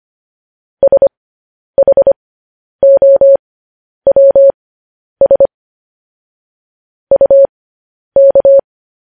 Morse code, heard as a keyed tone, about 25 wpm, spaced as for 10 wpm, 550 Hz. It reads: SHOWS UK
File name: podcast / News headlines-1x-25x10